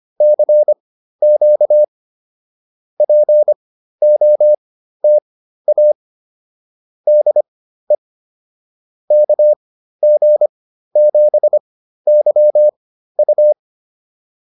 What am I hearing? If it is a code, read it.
Morse code, heard as a keyed tone, about 25 wpm, spaced as for 13 wpm, 600 Hz. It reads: CQ POTA DE KG7YU